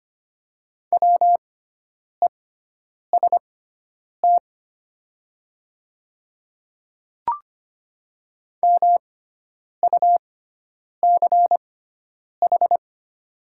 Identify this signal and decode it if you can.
Morse code, heard as a keyed tone, about 25 wpm, spaced as for 9 wpm, 700 Hz. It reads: WEST MUCH
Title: 6x Character Spacing - Top 500 Words - 25wpm